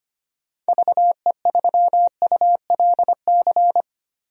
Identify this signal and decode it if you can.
Morse code, heard as a keyed tone, about 25 wpm, 700 Hz. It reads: VE3ULC
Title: DXCC-100 Call Signs - 6 Characters - Repeated 2x - 25wpm